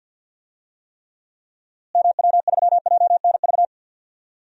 Morse code, heard as a keyed tone, about 46 wpm, 700 Hz. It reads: MW3JNV
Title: DXCC-100 Call Signs - 6 Characters - Repeated 3x - 50wpm